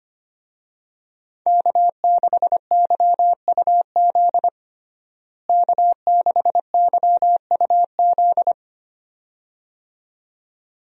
Morse code, heard as a keyed tone, about 25 wpm, 700 Hz. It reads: K6YUZ K6YUZ
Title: USA Call Signs - 5 Characters - Repeated 2x - 25wpm